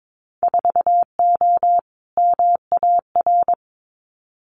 Morse code, heard as a keyed tone, about 22 wpm, 700 Hz. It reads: 4O MAR
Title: Winter Field Day Exchanges Encore 22wpm